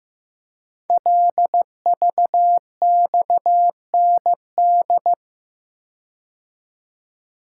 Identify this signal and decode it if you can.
Morse code, heard as a keyed tone, about 15 wpm, 700 Hz. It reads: LVXND
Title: Code Groups - 5 Letters - 15wpm